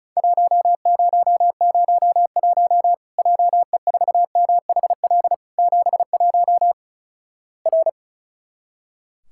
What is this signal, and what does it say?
Morse code, heard as a keyed tone, about 36 wpm, 700 Hz. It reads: 1001 JE4MHL 71 R
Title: Worked All Europe DX Contest - QTC Exchange - 35wpm